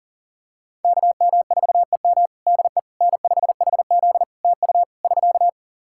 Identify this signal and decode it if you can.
Morse code, heard as a keyed tone, about 40 wpm, 700 Hz. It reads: KM4EM DE N5HZ TU <SK>
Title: Practice QSO - Exchange 4 - Ending - 40wpm